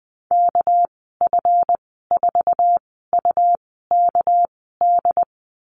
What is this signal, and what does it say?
Morse code, heard as a keyed tone, about 20 wpm, spaced as for 14 wpm, 700 Hz. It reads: KF4UKD
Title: 213 Lesson 34g - Number 8 - Call Signs 20wpm